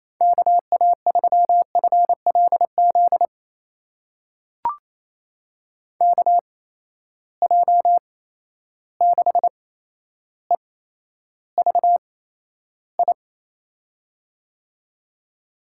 Morse code, heard as a keyed tone, about 28 wpm, spaced as for 12 wpm, 700 Hz. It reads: KA3FLZ KJ6EVI
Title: USA Call Signs - 6 Characters - 8x Character Spacing - 28wpm